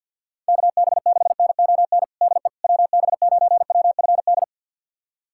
Morse code, heard as a keyed tone, about 47 wpm, 700 Hz. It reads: KB6NON DE WB9WFD
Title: USA Callsign DE Callsign 50wpm